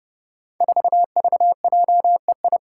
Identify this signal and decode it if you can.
Morse code, heard as a keyed tone, about 30 wpm, 700 Hz. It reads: 4VJEI
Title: Five Challenging Letter-Number Combinations 30wpm